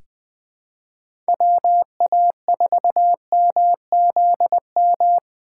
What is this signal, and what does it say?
Morse code, heard as a keyed tone, about 20 wpm, 700 Hz. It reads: WA4MZM
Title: USA Call Signs - 6 Characters - 3x Character Spacing - 20wpm